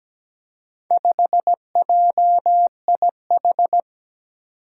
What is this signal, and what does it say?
Morse code, heard as a keyed tone, about 17 wpm, 700 Hz. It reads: 5JIH
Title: Four Challenging Letter-Number Combinations 17wpm